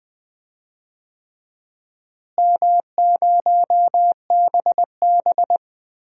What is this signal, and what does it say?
Morse code, heard as a keyed tone, about 20 wpm, 700 Hz. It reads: M0BB
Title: DXCC-100 Call Signs - 4 Characters - Repeated 3x - 20wpm